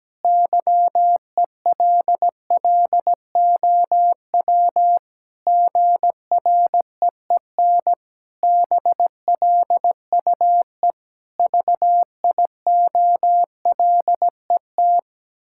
Morse code, heard as a keyed tone, about 17 wpm, 700 Hz. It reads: YELLOW GREEN BLUE VIOLET
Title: Pangram Challenge v2 - 17wpm